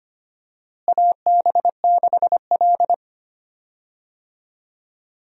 Morse code, heard as a keyed tone, about 25 wpm, 700 Hz. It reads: AB6L
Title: USA Call Signs - 4 Characters - Repeated 3x - 25wpm